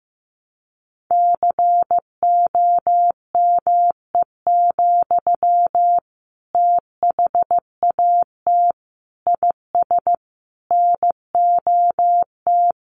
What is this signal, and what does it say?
Morse code, heard as a keyed tone, about 15 wpm, 700 Hz. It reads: COME, THAT IS NOT
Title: Sentences from Top 700 Words 15wpm